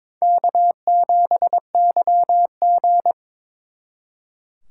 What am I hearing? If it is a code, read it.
Morse code, heard as a keyed tone, about 26 wpm, 700 Hz. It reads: K7YG
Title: Speed Racing - USA Call Signs 22wpm